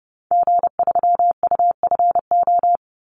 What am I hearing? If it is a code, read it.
Morse code, heard as a keyed tone, about 30 wpm, 700 Hz. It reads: G3UFO
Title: Call Signs - 30wpm